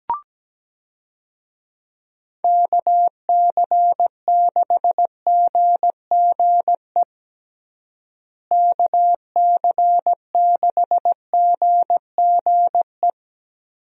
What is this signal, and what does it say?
Morse code, heard as a keyed tone, about 17 wpm, 700 Hz. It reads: KC6GGE KC6GGE
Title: USA Call Signs - 6 Characters - Repeated 3x - 17wpm